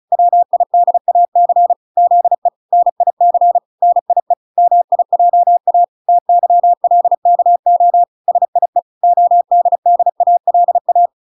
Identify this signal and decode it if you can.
Morse code, heard as a keyed tone, about 34 wpm, 700 Hz. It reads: WIDAC ZE NIC NIE MIJA TYLKO SIE ODDALA